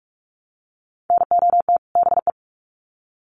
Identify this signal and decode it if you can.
Morse code, heard as a keyed tone, about 44 wpm, 700 Hz. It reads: NOT BE
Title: Sets of 2 Words 45wpm